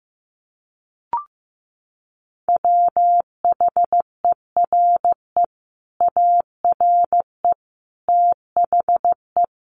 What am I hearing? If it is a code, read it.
Morse code, heard as a keyed tone, about 15 wpm, 700 Hz. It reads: WHERE ARE THE
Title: Sentences from Top 1200 Words 15wpm